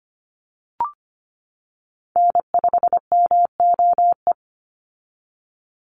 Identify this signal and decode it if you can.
Morse code, heard as a keyed tone, about 25 wpm, 700 Hz. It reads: N5MOE
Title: USA Call Signs 25wpm